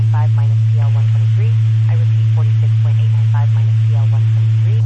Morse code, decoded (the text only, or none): none